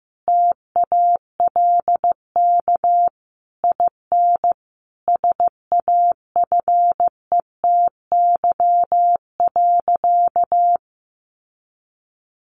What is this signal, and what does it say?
Morse code, heard as a keyed tone, about 15 wpm, 700 Hz. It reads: TALK IN SAFETY.
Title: Sentences from Top 700 Words 15wpm